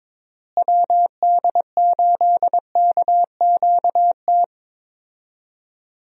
Morse code, heard as a keyed tone, about 22 wpm, 700 Hz. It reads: WD8KQT
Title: USA Call Signs - 6 Characters - Repeated 3x - 22wpm